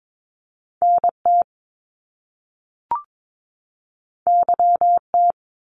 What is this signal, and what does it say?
Morse code, heard as a keyed tone, about 22 wpm, 700 Hz. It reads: NT YT
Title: Canadian Provinces and Territories 22wpm